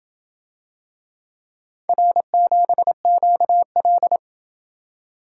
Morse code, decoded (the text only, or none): R7QL